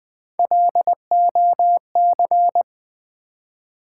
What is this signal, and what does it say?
Morse code, heard as a keyed tone, about 20 wpm, 700 Hz. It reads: LOC
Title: Build A Word - 3 Letters At A Time - 6-Letter Words - 20wpm